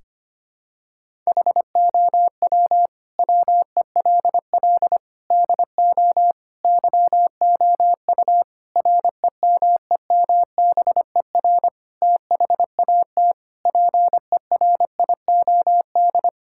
Morse code, heard as a keyed tone, about 25 wpm, 700 Hz. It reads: HOW WELL DO YOU REMEMBER THAT PERIOD
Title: Sentences from Top 1000 Words - Review Part 5 - 25wpm